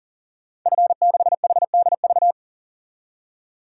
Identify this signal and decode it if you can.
Morse code, heard as a keyed tone, about 37 wpm, 700 Hz. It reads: F6HDV
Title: DXCC-100 Call Signs - 5 Characters - 40wpm